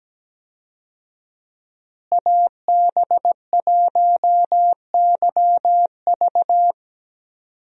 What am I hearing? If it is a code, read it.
Morse code, heard as a keyed tone, about 17 wpm, 700 Hz. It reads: AB1YV
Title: USA Call Signs - 5 Characters - Repeated 2x - 17wpm